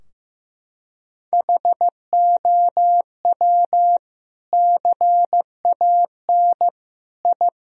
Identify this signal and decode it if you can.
Morse code, heard as a keyed tone, about 15 wpm, 700 Hz. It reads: HOW CAN I
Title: Sentences from Top 1000 Words - Review Part 2 - 15wpm